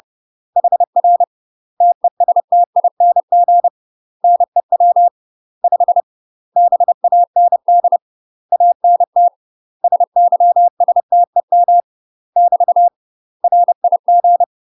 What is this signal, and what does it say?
Morse code, heard as a keyed tone, about 30 wpm, 700 Hz. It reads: HR TESTING NEW 5 BAND ANT SYSTEM <BT> RIG
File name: UK MASTER QSOs TRAINING / 30WPM / UK MASTER QSOs 0036